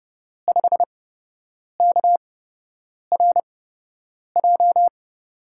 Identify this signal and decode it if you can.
Morse code, heard as a keyed tone, about 30 wpm, spaced as for 8 wpm, 700 Hz. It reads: 5KRJ
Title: USA Call Signs - 6 Characters - 8x Character Spacing - 30wpm